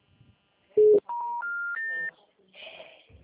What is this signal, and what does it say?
Special information tone.